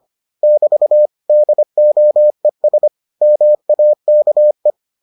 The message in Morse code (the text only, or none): <BT> DOES MAKE